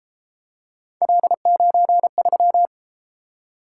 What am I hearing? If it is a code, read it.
Morse code, heard as a keyed tone, about 33 wpm, 700 Hz. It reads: L93